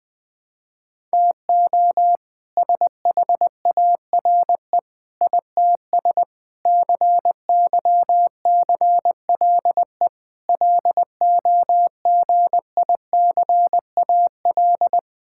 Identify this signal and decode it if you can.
Morse code, heard as a keyed tone, about 20 wpm, 700 Hz. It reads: TO SHARE ITS CYCLE LOGICAL
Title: All-Age Appropriate Jokes - 20wpm